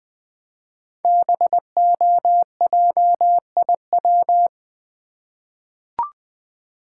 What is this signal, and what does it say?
Morse code, heard as a keyed tone, about 20 wpm, 700 Hz. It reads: BOJIW
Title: Code Groups - 5 Letters - 20wpm